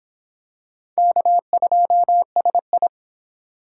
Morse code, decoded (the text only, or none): K2SI